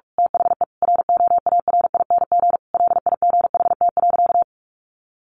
Morse code, heard as a keyed tone, about 45 wpm, 700 Hz. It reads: THE ROARING LIGHT.
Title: A Sound Of Thunder 45wpm